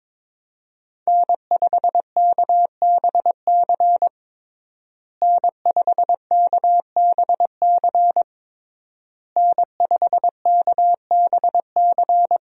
Morse code, heard as a keyed tone, about 21 wpm, 700 Hz. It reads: N5KBC N5KBC N5KBC
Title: USA Call Signs - 5 Characters - Repeated 3x - 22wpm